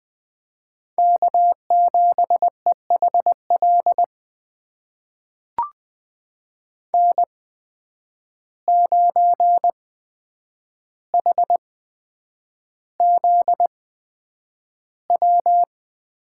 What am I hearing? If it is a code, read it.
Morse code, heard as a keyed tone, about 20 wpm, spaced as for 9 wpm, 700 Hz. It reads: K7EHL N9HZW
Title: USA Call Signs - 5 Characters - 8x Character Spacing - 20wpm